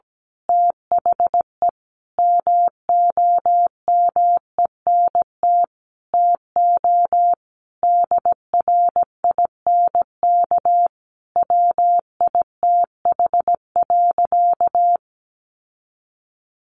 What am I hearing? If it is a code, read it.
Morse code, heard as a keyed tone, about 17 wpm, 700 Hz. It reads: THE MOMENT TO DRINK WITH.
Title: Sentences from Top 1500 Words 17wpm